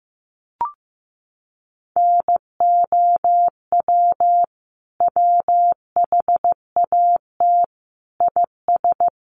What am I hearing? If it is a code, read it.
Morse code, heard as a keyed tone, about 14 wpm, 700 Hz. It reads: NOW WHAT IS
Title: Sentences from Top 200 Words 15wpm